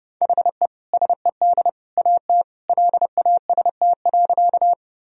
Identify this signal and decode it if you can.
Morse code, heard as a keyed tone, about 30 wpm, 700 Hz. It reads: HE SED AT LAST.
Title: Sentences from Top 300 Words and Abbreviations 30wpm